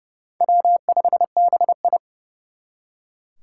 Morse code, heard as a keyed tone, about 37 wpm, 700 Hz. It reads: W5BI
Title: Speed Racing - USA Call Signs 30wpm